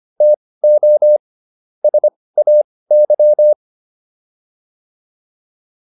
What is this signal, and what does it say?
Morse code, heard as a keyed tone, about 25 wpm, spaced as for 18 wpm, 600 Hz. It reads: TO SAY